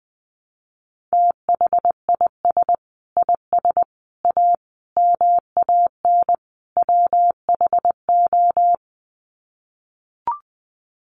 Morse code, heard as a keyed tone, about 20 wpm, 700 Hz. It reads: THIS IS A MAN WHO
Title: Sets of 5 Words Encore2 20wpm